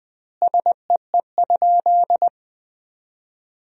Morse code, heard as a keyed tone, about 21 wpm, 700 Hz. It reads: SEE?
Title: 2x Word-Spacing - Sentences from Top 100 Words - 20wpm